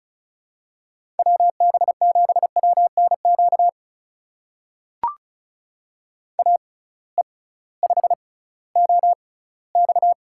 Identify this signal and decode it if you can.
Morse code, heard as a keyed tone, about 34 wpm, spaced as for 19 wpm, 700 Hz. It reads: WB7WNQ AE5OX
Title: USA Call Signs - 6x Character Spacing - 35wpm